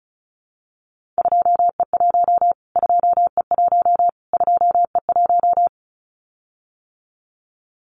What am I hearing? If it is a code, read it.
Morse code, heard as a keyed tone, about 34 wpm, 700 Hz. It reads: 2E1 2E1 2E1
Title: Callsign Prefixes for Top 25 DXCC - Repeated 3x - 35wpm